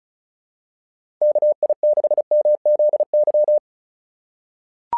Morse code, heard as a keyed tone, about 35 wpm, 600 Hz. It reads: KI6MZY
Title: SOTA QSO Practice 35wpm